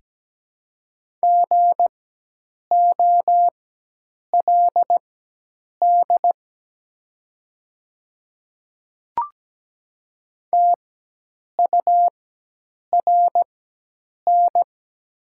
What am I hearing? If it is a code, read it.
Morse code, heard as a keyed tone, about 17 wpm, spaced as for 8 wpm, 700 Hz. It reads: GOLD TURN
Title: 4x Character Spacing - Top 500 Words - 17wpm